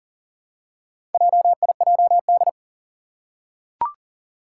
Morse code, heard as a keyed tone, about 42 wpm, 700 Hz. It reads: JIJD
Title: Four Challenging Letter-Number Combinations 40wpm